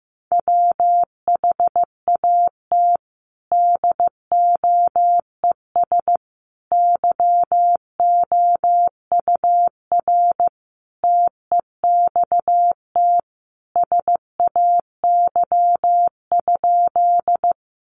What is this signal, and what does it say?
Morse code, heard as a keyed tone, about 15 wpm, 700 Hz. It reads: WHAT DOES YOUR TEXT SAY?
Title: Sentences from Top 1200 Words 15wpm